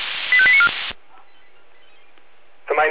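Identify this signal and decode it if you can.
Five-tone selective call (ZVEI), heard as quick runs of five tones, tones of about 70 ms. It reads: 85894
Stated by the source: ZVEI five-tone selective call received on an SDR.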